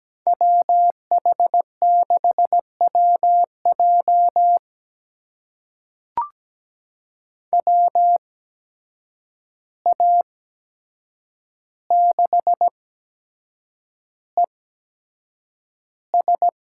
Morse code, heard as a keyed tone, about 17 wpm, spaced as for 7 wpm, 700 Hz. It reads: WH6WJ WA6ES
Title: USA Call Signs - 5 Characters - 8x Character Spacing - 17wpm